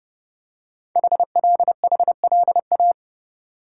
Morse code, heard as a keyed tone, about 31 wpm, 690 Hz. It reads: HLHLA
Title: Code Groups - 5 Letters - 30wpm